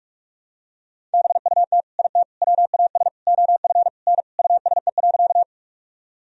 Morse code, heard as a keyed tone, about 46 wpm, 700 Hz. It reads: BUT IT WAS OF N USE.
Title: Sentences from Top 100 Words and Abbreviations 45wpm